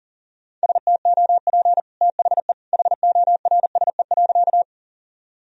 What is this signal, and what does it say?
Morse code, heard as a keyed tone, about 40 wpm, 700 Hz. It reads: STOP THE HORSE.
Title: Sentences from Top 1000 Words - Review Part 1 - 40wpm